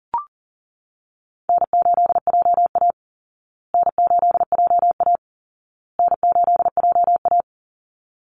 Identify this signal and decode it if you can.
Morse code, heard as a keyed tone, about 40 wpm, 700 Hz. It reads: N8JA N8JA N8JA
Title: Call Signs Repeated 2x - 40wpm